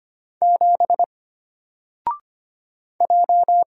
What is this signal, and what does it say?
Morse code, heard as a keyed tone, about 25 wpm, 700 Hz. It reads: 7 J
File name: Single-Letter-Number-25wpm